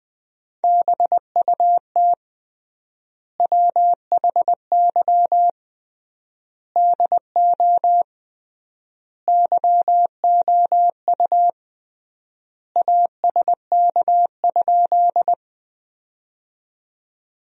Morse code, heard as a keyed tone, about 21 wpm, 700 Hz. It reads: BUT WHY DO YOU ASK?
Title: 3x Word-Spacing - Sentences from Top 200 Words - 20wpm